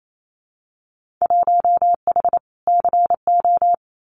Morse code, heard as a keyed tone, about 28 wpm, 700 Hz. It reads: 1H CO
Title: Winter Field Day Exchanges 28wpm